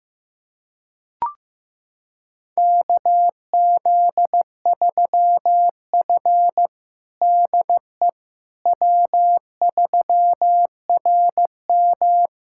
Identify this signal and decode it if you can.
Morse code, heard as a keyed tone, about 15 wpm, 700 Hz. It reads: KZ3F DE W3RM
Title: USA Callsign DE Callsign - 4 Characters - 15wpm